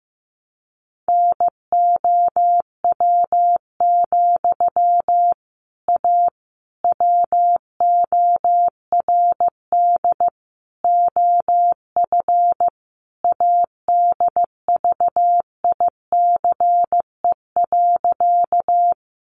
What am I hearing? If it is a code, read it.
Morse code, heard as a keyed tone, about 15 wpm, 700 Hz. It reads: NOW, A WORD OF ADVICE.
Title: Sentences from Top 1100 Words 15wpm